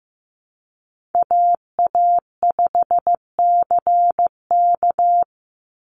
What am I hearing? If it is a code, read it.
Morse code, heard as a keyed tone, about 15 wpm, 700 Hz. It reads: AA5CK
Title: NA SOTA Chaser Callsigns 15wpm